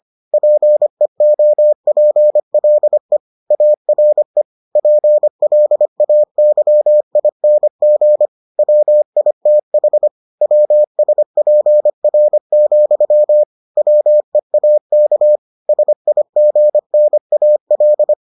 Morse code, heard as a keyed tone, about 25 wpm, 600 Hz. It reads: PEOPLE ARE PLAYING WITH WSPR, WEAK SIGNAL